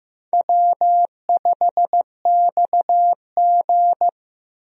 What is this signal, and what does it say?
Morse code, heard as a keyed tone, about 15 wpm, 700 Hz. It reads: W5XG
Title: USA Call Signs - 4 Characters - Repeated 2x - 15wpm